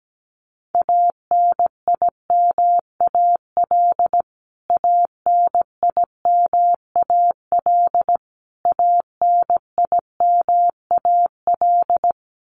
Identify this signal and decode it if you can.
Morse code, heard as a keyed tone, about 17 wpm, 700 Hz. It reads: ANIMAL ANIMAL ANIMAL
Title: Top 300 Words Repeated 17wpm